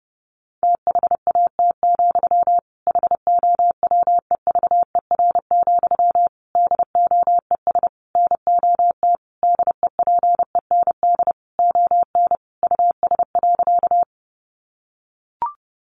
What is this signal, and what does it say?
Morse code, heard as a keyed tone, about 30 wpm, 700 Hz. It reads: THAT, HOWEVER, DOES NOT DEPEND ON US.
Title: Sentences from Top 1300 Words 30wpm